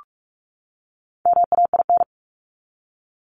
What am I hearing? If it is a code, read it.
Morse code, heard as a keyed tone, about 44 wpm, 700 Hz. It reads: MAIN
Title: LY Suffix - Top 100 Words with Base Words 45wpm